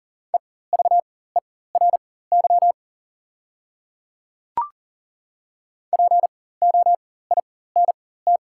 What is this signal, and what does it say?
Morse code, heard as a keyed tone, about 39 wpm, spaced as for 19 wpm, 690 Hz. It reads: EVERY POINT
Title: 4x Character Spacing - Top 500 Words - 40wpm